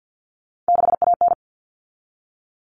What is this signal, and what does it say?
Morse code, heard as a keyed tone, about 50 wpm, 700 Hz. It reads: 6AN
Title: All Cut Numbers - 3-Digit 50wpm